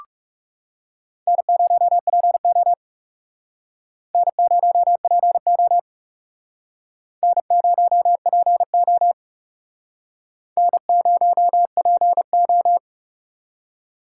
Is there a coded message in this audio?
Yes: Morse code from 1.3 to 12.8 s, heard as a keyed tone, about 40 wpm, 700 Hz. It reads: N0PO N0PO N0PO N0PO